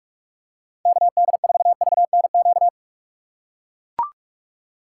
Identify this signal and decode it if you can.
Morse code, heard as a keyed tone, about 41 wpm, 700 Hz. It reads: KD4VNQ
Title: USA Call Signs - 6 Characters - 4x Character Spacing - 45wpm